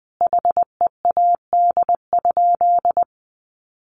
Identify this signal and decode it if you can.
Morse code, heard as a keyed tone, about 20 wpm, 700 Hz. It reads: HEAD?
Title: Sentences from Top 300 Words 20wpm